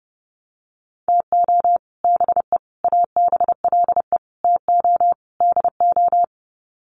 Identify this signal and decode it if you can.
Morse code, heard as a keyed tone, about 30 wpm, 700 Hz. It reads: TO BE ABLE TO DO
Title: Sets of 5 Words 30wpm